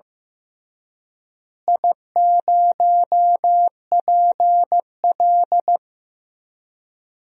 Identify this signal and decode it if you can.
Morse code, heard as a keyed tone, about 15 wpm, 700 Hz. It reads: I0PL
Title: DXCC-100 Call Signs - 4 Characters - Repeated 3x - 15wpm